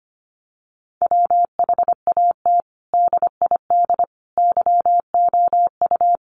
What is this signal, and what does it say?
Morse code, heard as a keyed tone, about 25 wpm, 700 Hz. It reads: WHAT DID YOU
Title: Sentences from Top 100 Words 25wpm